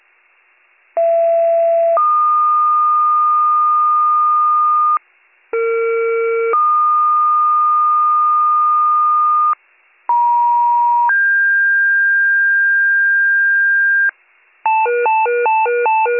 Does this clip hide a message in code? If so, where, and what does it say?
Yes: two-tone page from 1.0 to 14.1 s, heard as a one-second tone, then a three-second tone, about 670 then 1220 Hz, about 460 then 1220 Hz, about 950 then 1600 Hz.